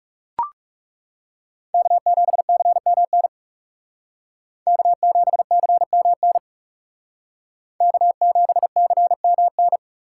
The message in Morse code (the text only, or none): K7CMN K7CMN K7CMN